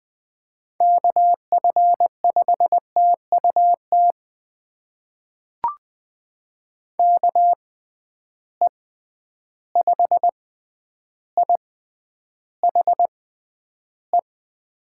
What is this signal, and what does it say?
Morse code, heard as a keyed tone, about 20 wpm, spaced as for 10 wpm, 700 Hz. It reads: KF5TUT KE5IHE
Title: USA Call Signs - 6 Characters - 6x Character Spacing - 20wpm